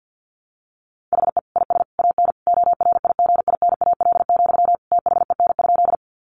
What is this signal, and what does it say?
Morse code, heard as a keyed tone, about 50 wpm, 700 Hz. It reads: HE IS AN ORIGINAL, THEN?
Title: Sentences from Top 900 Words 50wpm